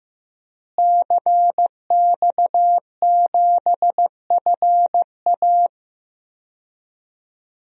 Morse code, heard as a keyed tone, about 15 wpm, 700 Hz. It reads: CX7FA
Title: DXCC-100 Call Signs - 5 Characters - Repeated 3x - 15wpm